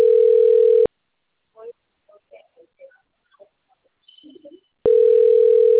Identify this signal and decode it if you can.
Ringback tone (North America), 440 + 480 Hz.